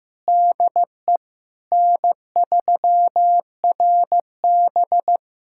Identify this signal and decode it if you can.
Morse code, heard as a keyed tone, about 15 wpm, 700 Hz. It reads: DE N3RB
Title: SOTA QSO Practice 15wpm